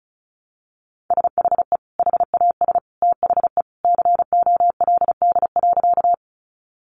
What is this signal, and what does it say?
Morse code, heard as a keyed tone, about 34 wpm, 700 Hz. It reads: SHE HAS THE COLD.
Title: Easy Phrases 35wpm